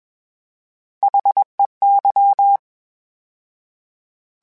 Morse code, heard as a keyed tone, about 21 wpm, 800 Hz. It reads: HEY
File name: H Prefixes / H mp3s